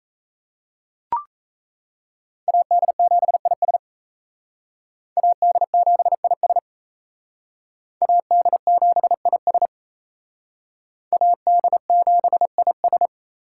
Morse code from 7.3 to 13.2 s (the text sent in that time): AD7IS AD7IS